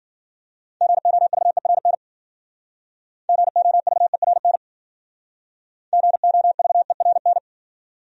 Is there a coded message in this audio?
Yes: Morse code from 0.8 to 7.4 s, heard as a keyed tone, about 53 wpm, 700 Hz. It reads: GOVERN GOVERN GOVERN